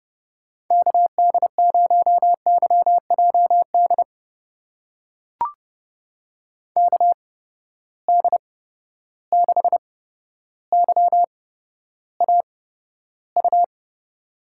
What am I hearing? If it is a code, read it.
Morse code, heard as a keyed tone, about 30 wpm, spaced as for 13 wpm, 700 Hz. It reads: KD0YJD KD6YAU
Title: USA Call Signs - 6 Characters - 8x Character Spacing - 30wpm